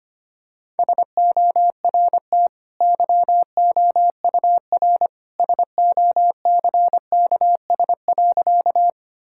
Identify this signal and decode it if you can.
Morse code, heard as a keyed tone, about 25 wpm, 700 Hz. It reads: SORT YOUR SOCKS.
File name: Easy Sentences with Hints - 25wpm